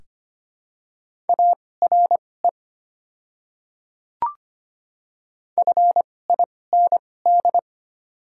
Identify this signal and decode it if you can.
Morse code, heard as a keyed tone, about 25 wpm, spaced as for 18 wpm, 690 Hz. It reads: ARE FIND
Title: 2x Character Spacing - Top 200 Words - 25wpm